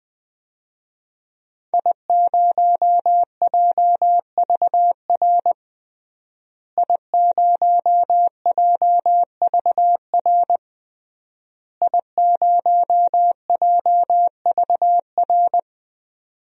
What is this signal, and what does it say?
Morse code, heard as a keyed tone, about 20 wpm, 700 Hz. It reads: I0JVR I0JVR I0JVR
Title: DXCC-100 Call Signs - 5 Characters - Repeated 3x - 20wpm